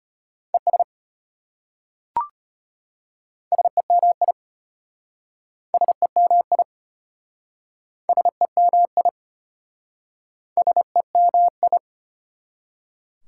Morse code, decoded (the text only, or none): ES SEMI SEMI SEMI SEMI